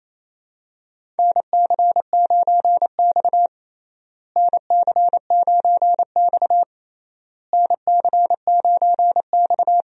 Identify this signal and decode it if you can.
Morse code, heard as a keyed tone, about 28 wpm, 700 Hz. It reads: NC9X NC9X NC9X
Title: USA Call Signs - 4 Characters - Repeated 3x - 28wpm